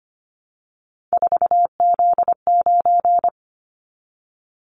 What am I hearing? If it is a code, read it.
Morse code, heard as a keyed tone, about 25 wpm, 700 Hz. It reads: 4Z9
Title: Callsign Prefixes for Top 50 DXCC - 25wpm